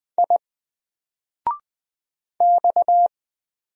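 Morse code, heard as a keyed tone, about 20 wpm, 700 Hz. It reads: I X
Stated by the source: Single letter and number Morse drill at 20 wpm, from N4LCD.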